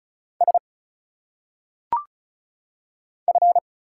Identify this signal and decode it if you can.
Morse code, heard as a keyed tone, about 34 wpm, 700 Hz. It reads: S F
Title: Single Letters - Phonetic Pronunciation 35wpm